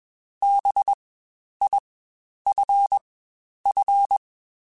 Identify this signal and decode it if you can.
Morse code, heard as a keyed tone, about 21 wpm, spaced as for 10 wpm, 800 Hz. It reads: BIFF
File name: Spaced Alphas / Spaced Alphas mp3s / Bs sp @20